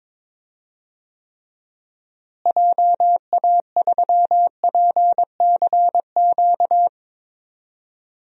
Morse code, heard as a keyed tone, about 21 wpm, 700 Hz. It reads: JA3PCQ
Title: DXCC-100 Call Signs - 6 Characters - Repeated 3x - 22wpm